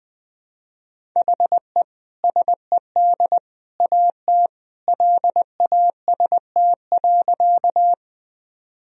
Morse code, heard as a keyed tone, about 20 wpm, 700 Hz. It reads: HE SED AT LAST.
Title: Sentences from Top 300 Words and Abbreviations 20wpm